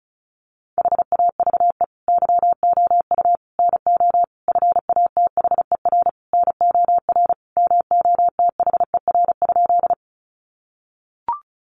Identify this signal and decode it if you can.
Morse code, heard as a keyed tone, about 34 wpm, 700 Hz. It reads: HAVE YOU NO FATHER NOR MOTHER?
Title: Sentences from Top 1500 Words 35wpm